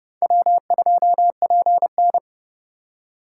Morse code, heard as a keyed tone, about 31 wpm, 700 Hz. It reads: W2PN